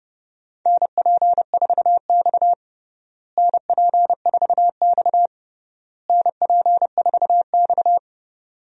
Morse code, heard as a keyed tone, about 30 wpm, 700 Hz. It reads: NP4X NP4X NP4X
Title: USA Call Signs - 4 Characters - Repeated 3x - 30wpm